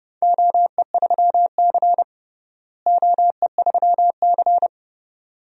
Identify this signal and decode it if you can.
Morse code, heard as a keyed tone, about 30 wpm, 700 Hz. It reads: OE3C OE3C
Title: DXCC-100 Call Signs - 4 Characters - Repeated 2x - 30wpm